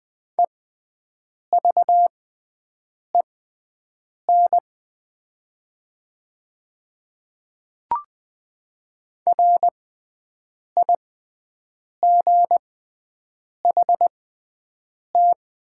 Morse code, heard as a keyed tone, about 21 wpm, spaced as for 7 wpm, 700 Hz. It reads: EVEN RIGHT
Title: 6x Character Spacing - Top 300 Words - 20wpm